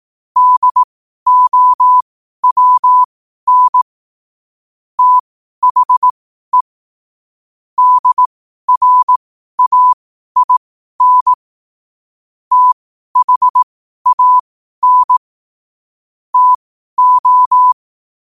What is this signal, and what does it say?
Morse code, heard as a keyed tone, about 18 wpm, spaced as for 12 wpm, 1000 Hz. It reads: DOWN THE DRAIN THAN TO